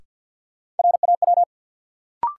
Morse code, heard as a keyed tone, about 49 wpm, 700 Hz. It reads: RAW
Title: Sentences from Top 1500 Words - New Word Drill - 50wpm